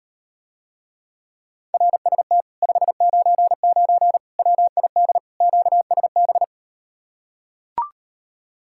Morse code, heard as a keyed tone, about 38 wpm, 700 Hz. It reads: RST 599 WID QSB